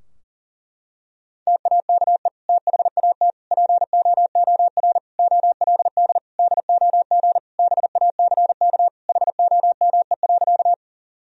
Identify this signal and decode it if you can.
Morse code, heard as a keyed tone, about 37 wpm, 700 Hz. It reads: TAKE THAT POOR OLD DOG BACK HOME.
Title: Sentences from Top 1000 Words - Review Part 2 - 40wpm